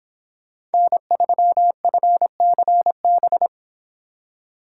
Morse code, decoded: N3FCB

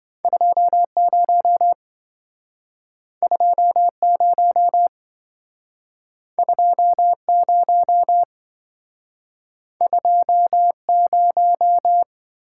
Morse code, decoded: 20 20 20 20

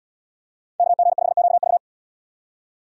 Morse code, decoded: KG5QV

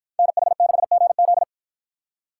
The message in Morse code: NH6GZ

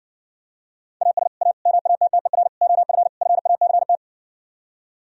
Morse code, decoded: AS A MATTER OF FACT

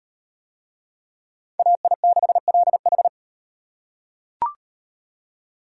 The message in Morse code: AI6LH

